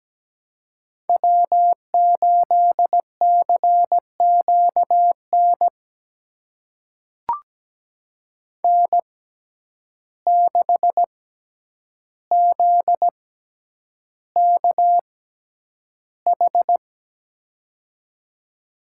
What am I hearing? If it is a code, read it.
Morse code, heard as a keyed tone, about 17 wpm, spaced as for 9 wpm, 700 Hz. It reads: W8CQN N6ZKH
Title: USA Call Signs - 5 Characters - 6x Character Spacing - 17wpm